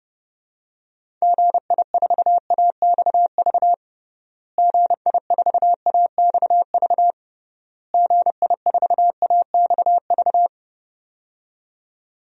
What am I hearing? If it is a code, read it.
Morse code, heard as a keyed tone, about 28 wpm, 700 Hz. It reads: GI4AXV GI4AXV GI4AXV
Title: DXCC-100 Call Signs - 6 Characters - Repeated 3x - 30wpm